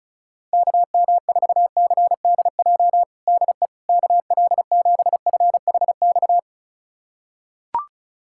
Morse code, heard as a keyed tone, about 34 wpm, 700 Hz. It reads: KM4CDJ DE KL7FHX
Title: USA Callsign DE Callsign - 6 Characters - 35wpm